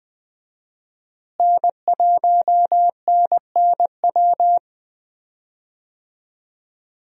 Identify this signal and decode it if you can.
Morse code, heard as a keyed tone, about 20 wpm, 700 Hz. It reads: N1NNW
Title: USA Call Signs - 5 Characters - Repeated 3x - 20wpm